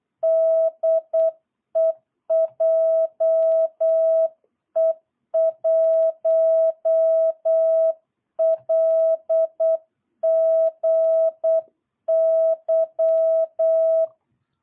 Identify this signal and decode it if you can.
Morse code, heard as a keyed tone, about 8 wpm, 660 Hz. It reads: DE JE1LGY
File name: jh8cbh / 2022cw